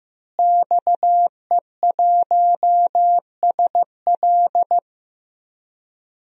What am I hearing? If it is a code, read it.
Morse code, heard as a keyed tone, about 15 wpm, 700 Hz. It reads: XE1SL
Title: DXCC-100 Call Signs - 5 Characters - Repeated 3x - 15wpm